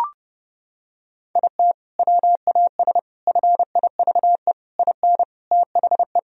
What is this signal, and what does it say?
Morse code, heard as a keyed tone, about 30 wpm, 700 Hz. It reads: IT WAS FIVE IN THE
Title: Sentences from Top 1000 Words - Review Part 2 - 30wpm